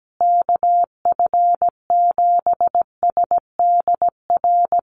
Morse code, heard as a keyed tone, about 17 wpm, 700 Hz. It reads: KF7SDR
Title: USA Call Signs Repeated 2x 17wpm